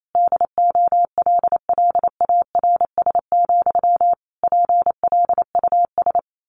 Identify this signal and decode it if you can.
Morse code, heard as a keyed tone, about 28 wpm, 700 Hz. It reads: DOLLARS, PLUS
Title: A Sound Of Thunder 28wpm